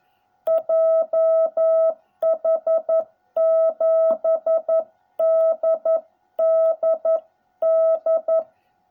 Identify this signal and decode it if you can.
Morse code, heard as a keyed tone, about 11 wpm, 660 Hz. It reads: JH7DDD